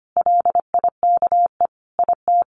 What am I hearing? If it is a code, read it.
Morse code, heard as a keyed tone, about 25 wpm, 700 Hz. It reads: LIKE IT
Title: Sentences from Top 1400 Words 25wpm